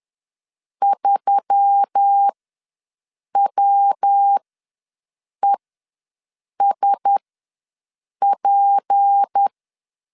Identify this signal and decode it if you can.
Morse code, heard as a keyed tone, about 10 wpm, spaced as for 6 wpm, 800 Hz. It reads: 3WESP
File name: LEZIONI / 6 PKX0 / 6 P ws 10wpm 3751